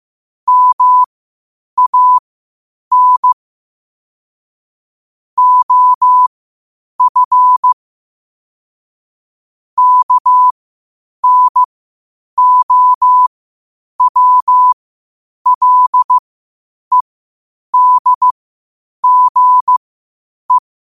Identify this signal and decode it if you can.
Morse code, heard as a keyed tone, about 15 wpm, spaced as for 8 wpm, 1000 Hz. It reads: MAN OF KNOWLEDGE
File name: qotdfiles / 8wpm